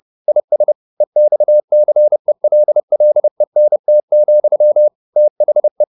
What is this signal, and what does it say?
Morse code, heard as a keyed tone, about 30 wpm, 600 Hz. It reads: IS EXCELLENT, THE
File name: the old man and the sea / the old man and the sea 1 30wpm